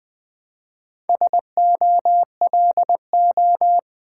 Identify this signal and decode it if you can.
Morse code, heard as a keyed tone, about 20 wpm, 700 Hz. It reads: SOLO
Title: Build A Word - 2 Letters At A Time - 4-Letter Words - 20wpm